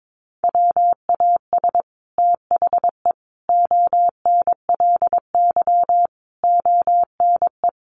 Morse code, heard as a keyed tone, about 22 wpm, 700 Hz. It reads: WAS THE ONLY ONE
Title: Sets of 5 Words Encore2 22wpm